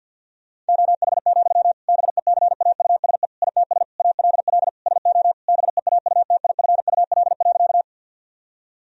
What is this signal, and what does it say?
Morse code, heard as a keyed tone, about 50 wpm, 700 Hz. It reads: OH, BECAUSE ITS ALL SO BEAUTIFUL.
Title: Sentences from Top 1000 Words - Review Part 3 - 50wpm